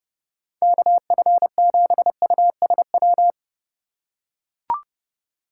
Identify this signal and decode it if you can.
Morse code, heard as a keyed tone, about 31 wpm, 700 Hz. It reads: KF7USW
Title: USA Call Signs - 6 Characters - 4x Character Spacing - 30wpm